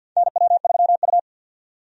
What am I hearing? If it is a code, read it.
Morse code, heard as a keyed tone, about 52 wpm, 700 Hz. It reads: NW3U
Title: USA Call Signs - 4 Characters - 50wpm